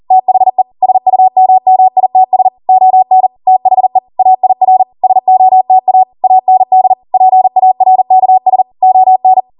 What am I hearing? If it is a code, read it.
Morse code, heard as a keyed tone, about 40 wpm, 750 Hz. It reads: THE SUMMITS ON THE AIR SOTA AND PARKS ON